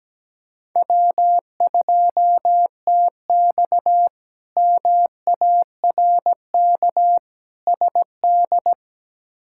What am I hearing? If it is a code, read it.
Morse code, heard as a keyed tone, about 17 wpm, 700 Hz. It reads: W2TX MARK SD
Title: K1USN SST QSO Practice - 17wpm